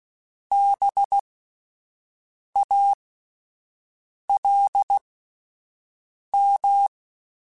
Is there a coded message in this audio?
Yes: Morse code from 0.5 to 6.9 s, heard as a keyed tone, about 16 wpm, spaced as for 5 wpm, 800 Hz. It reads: BALM